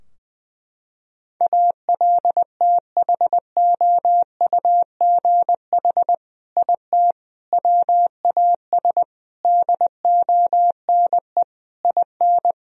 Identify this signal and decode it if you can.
Morse code, heard as a keyed tone, about 21 wpm, 700 Hz. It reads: ALTHOUGH IT WAS DONE IN
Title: Sentences from Top 1000 Words - Review Part 3 - 20wpm